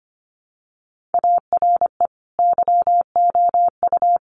Morse code, heard as a keyed tone, about 25 wpm, 700 Hz. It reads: ARE YOU
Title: Sentences from Top 1500 Words 25wpm